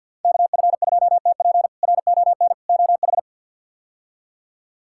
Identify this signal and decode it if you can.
Morse code, heard as a keyed tone, about 50 wpm, 700 Hz. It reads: KF2TP RON OH
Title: K1USN SST QSO Practice - 50wpm